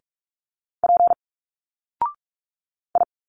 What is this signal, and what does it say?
Morse code, heard as a keyed tone, about 45 wpm, 700 Hz. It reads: P I
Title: Single Letters 45wpm